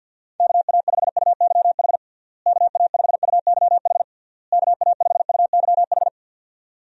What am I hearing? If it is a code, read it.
Morse code, heard as a keyed tone, about 51 wpm, 700 Hz. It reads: KA5UYH KA5UYH KA5UYH
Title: USA Call Signs - 6 Characters - Repeated 3x - 50wpm